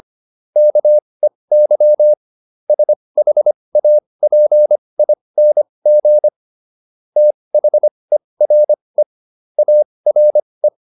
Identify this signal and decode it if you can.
Morse code, heard as a keyed tone, about 25 wpm, spaced as for 20 wpm, 600 Hz. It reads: KEY SHAPING THERE ARE